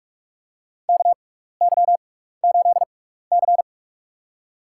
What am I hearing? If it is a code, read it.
Morse code, heard as a keyed tone, about 42 wpm, spaced as for 15 wpm, 700 Hz. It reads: KY8C
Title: USA Call Signs - 6x Character Spacing - 45wpm